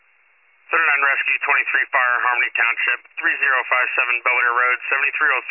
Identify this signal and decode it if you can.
Emergency Alert System (SAME) tones, heard as harsh buzzing bursts, with speech. It is heard from 0.8 to 4.8 s.